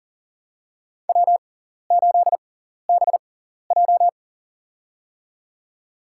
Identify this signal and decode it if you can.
Morse code, heard as a keyed tone, about 40 wpm, spaced as for 14 wpm, 700 Hz. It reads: W8BJ
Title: USA Call Signs - 4 Characters - 6x Character Spacing - 40wpm